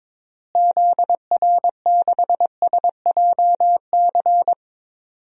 Morse code, heard as a keyed tone, about 22 wpm, 700 Hz. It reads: ZR6SJC
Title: DXCC-100 Call Signs - 6 Characters - Repeated 3x - 22wpm